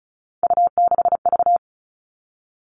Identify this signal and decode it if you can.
Morse code, heard as a keyed tone, about 34 wpm, 700 Hz. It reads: U6V
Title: All Cut Numbers - 3-Digit 35wpm